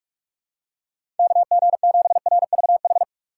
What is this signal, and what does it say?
Morse code, heard as a keyed tone, about 46 wpm, 700 Hz. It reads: KG7RVH
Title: USA Call Signs - 6 Characters - 4x Character Spacing - 45wpm